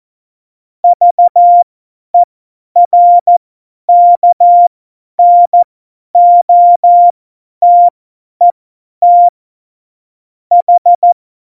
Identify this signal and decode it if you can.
Morse code, heard as a keyed tone, about 14 wpm, spaced as for 10 wpm, 700 Hz. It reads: VERKNOTET H
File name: Morsekurs / Witze / Witz 17. November 2020, 50 F70